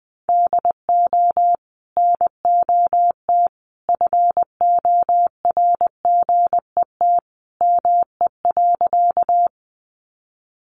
Morse code, heard as a keyed tone, about 20 wpm, 700 Hz. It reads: DO NOT FORGET ME.
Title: Sentences from Top 1200 Words 20wpm